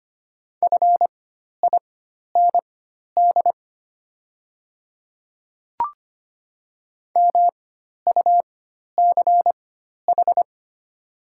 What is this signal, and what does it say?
Morse code, heard as a keyed tone, about 25 wpm, spaced as for 12 wpm, 700 Hz. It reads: FIND MUCH
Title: 4x Character Spacing - Top 200 Words - 25wpm